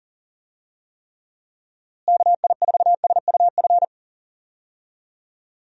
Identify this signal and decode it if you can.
Morse code, heard as a keyed tone, about 37 wpm, 700 Hz. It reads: KI4SUF
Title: USA Call Signs - 6 Characters - Repeated 2x - 40wpm